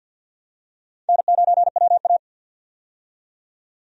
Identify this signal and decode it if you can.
Morse code, heard as a keyed tone, about 48 wpm, 700 Hz. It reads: N9WA
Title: USA Call Signs - 4 Characters - 50wpm